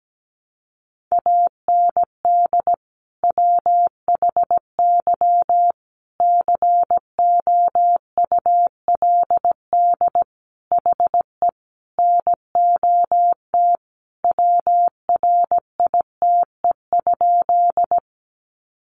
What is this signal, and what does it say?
Morse code, heard as a keyed tone, about 17 wpm, 700 Hz. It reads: AND WHY COULD HE NOT WRITE?
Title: Sentences from Top 600 Words 17wpm